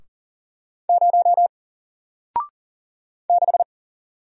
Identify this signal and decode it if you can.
Morse code, heard as a keyed tone, about 40 wpm, 700 Hz. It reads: 0 6
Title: Single Letter-Number 40wpm